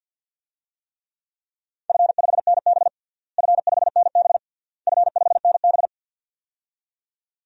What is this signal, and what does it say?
Morse code, heard as a keyed tone, about 53 wpm, 700 Hz. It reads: F5NB F5NB F5NB
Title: DXCC-100 Call Signs - 4 Characters - Repeated 3x - 50wpm